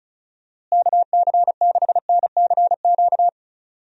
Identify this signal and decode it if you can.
Morse code, heard as a keyed tone, about 36 wpm, 700 Hz. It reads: KC6NCQ
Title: USA Call Signs - 6 Characters - Repeated 2x - 35wpm